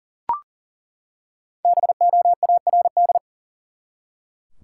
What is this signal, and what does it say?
Morse code, heard as a keyed tone, about 40 wpm, 700 Hz. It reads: BOARD
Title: Five Letter Words - 40wpm